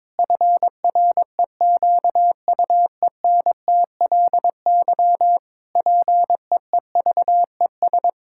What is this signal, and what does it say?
Morse code, heard as a keyed tone, about 22 wpm, 700 Hz. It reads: FREQUENTLY PEEVES
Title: Pangram Challenge v2 - 22wpm